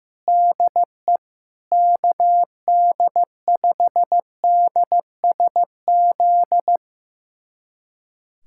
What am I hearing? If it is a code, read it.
Morse code, heard as a keyed tone, about 15 wpm, 700 Hz. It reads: DE KD5DSZ
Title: USA Callsign DE Callsign 15wpm